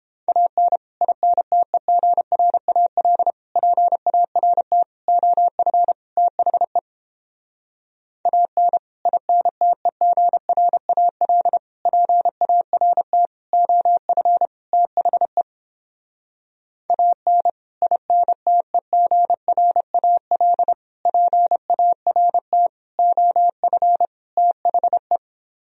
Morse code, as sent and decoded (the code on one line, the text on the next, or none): .- -. / .. -. - . --. .-. .- .-.. / .--. .- .-. - / --- ..-. / - .... / .- -. / .. -. - . --. .-. .- .-.. / .--. .- .-. - / --- ..-. / - .... / .- -. / .. -. - . --. .-. .- .-.. / .--. .- .-. - / --- ..-. / - .... .
AN INTEGRAL PART OF THE AN INTEGRAL PART OF THE AN INTEGRAL PART OF THE